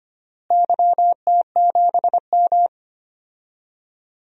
Morse code, heard as a keyed tone, about 25 wpm, 700 Hz. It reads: YT7M
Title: DXCC-100 Call Signs - 4 Characters - Repeated 3x - 25wpm